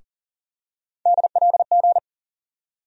Morse code, heard as a keyed tone, about 40 wpm, 700 Hz. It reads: DLG
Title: USA Call Sign Postfixes - 3 Letters 40wpm